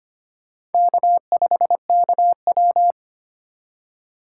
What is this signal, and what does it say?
Morse code, heard as a keyed tone, about 32 wpm, 700 Hz. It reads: K5KW